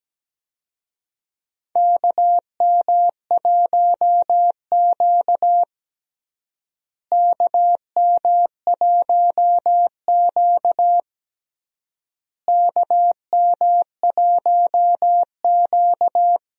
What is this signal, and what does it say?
Morse code, heard as a keyed tone, about 17 wpm, 700 Hz. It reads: KM1Q KM1Q KM1Q
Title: USA Call Signs - 4 Characters - Repeated 3x - 17wpm